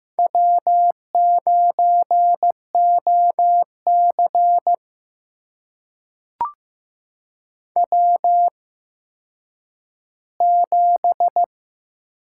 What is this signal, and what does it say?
Morse code, heard as a keyed tone, about 15 wpm, spaced as for 9 wpm, 700 Hz. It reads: W9OC W7
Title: USA Call Signs - 4 Characters - 8x Character Spacing - 15wpm